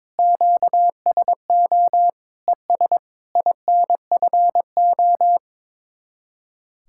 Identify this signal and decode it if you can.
Morse code, heard as a keyed tone, about 22 wpm, 700 Hz. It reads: QSO ES INFO
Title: Speed-Racing Common QSO Phrases - 22wpm